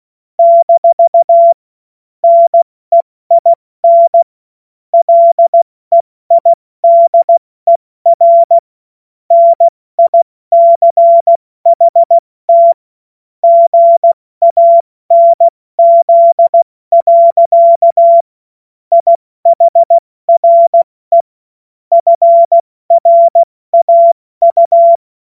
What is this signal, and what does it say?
Morse code, heard as a keyed tone, about 16 wpm, 680 Hz. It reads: - NEIN LEIDER NICHT GANZ. IHRE FRAU
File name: Morsekurs / Witze / Witz 17.Oktober 2023 70 F80